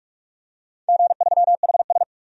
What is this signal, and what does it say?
Morse code, heard as a keyed tone, about 46 wpm, 700 Hz. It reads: G3HS
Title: DXCC-100 Call Signs - 4 Characters - 45wpm